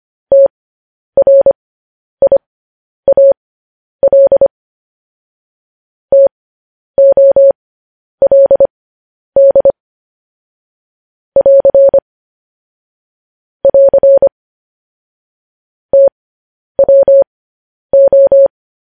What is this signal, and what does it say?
Morse code, heard as a keyed tone, about 25 wpm, spaced as for 10 wpm, 550 Hz. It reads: TRIAL TOLD <AR> <AR> TWO
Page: news-headlines-in-morse-code-at-10-wpm